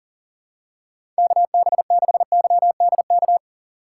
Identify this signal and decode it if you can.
Morse code, heard as a keyed tone, about 40 wpm, 700 Hz. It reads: KB6YDK
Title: USA Call Signs - 6 Characters - 2x Character Spacing - 40wpm